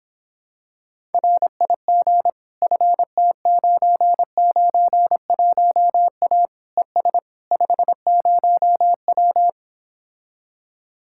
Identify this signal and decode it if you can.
Morse code, heard as a keyed tone, about 26 wpm, 700 Hz. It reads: RIG FT991A ES 50W